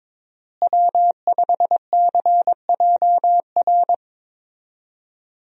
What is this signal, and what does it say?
Morse code, heard as a keyed tone, about 22 wpm, 700 Hz. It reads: W5CJR
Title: USA Call Signs - 5 Characters - Repeated 3x - 22wpm